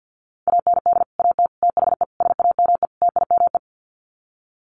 Morse code, heard as a keyed tone, about 50 wpm, 700 Hz. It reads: AND AT THE SAME TIME
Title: Sets of 5 Words 50wpm